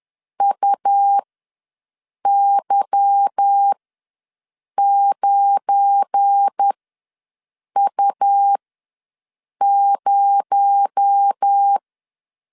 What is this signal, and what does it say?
Morse code, heard as a keyed tone, about 10 wpm, spaced as for 6 wpm, 800 Hz. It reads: UY9U0
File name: LEZIONI / 8 YJ45 / 8 Y ws 10wpm 3952